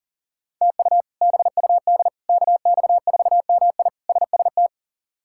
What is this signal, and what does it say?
Morse code, heard as a keyed tone, about 37 wpm, 700 Hz. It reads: TU BUD KX4MI SST
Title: K1USN SST QSO Practice - 40wpm